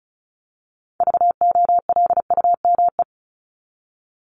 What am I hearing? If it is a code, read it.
Morse code, heard as a keyed tone, about 34 wpm, 700 Hz. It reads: VOLUME
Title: Top 1751-2000 Words 35wpm